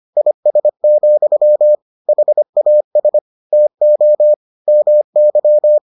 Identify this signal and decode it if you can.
Morse code, heard as a keyed tone, about 25 wpm, 600 Hz. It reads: IS, HAS TO MY